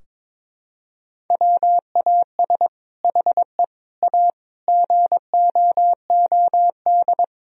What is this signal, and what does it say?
Morse code, heard as a keyed tone, about 22 wpm, 700 Hz. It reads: WAS HE A GOOD
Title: Sentences from Top 1000 Words - Review Part 4 - 22wpm